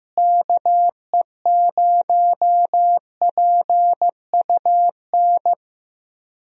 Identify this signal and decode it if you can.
Morse code, heard as a keyed tone, about 16 wpm, 700 Hz. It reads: KE0PUN
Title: USA Call Signs - 6 Characters - Repeated 2x - 15wpm